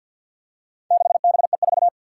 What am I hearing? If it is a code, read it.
Morse code, heard as a keyed tone, about 52 wpm, 700 Hz. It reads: 6BE4
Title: Four Challenging Letter-Number Combinations 50wpm